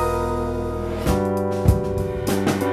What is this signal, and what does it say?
There is no code.